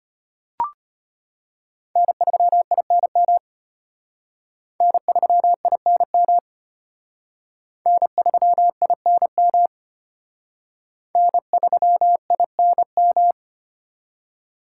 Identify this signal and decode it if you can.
Morse code, heard as a keyed tone, about 32 wpm, 700 Hz. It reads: N3INM N3INM N3INM N3INM